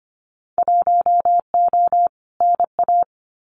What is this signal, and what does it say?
Morse code, heard as a keyed tone, about 25 wpm, 700 Hz. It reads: 1O NA
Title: Winter Field Day Exchanges 25wpm